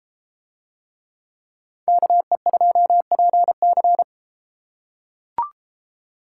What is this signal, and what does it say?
Morse code, heard as a keyed tone, about 32 wpm, 700 Hz. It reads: KE2PC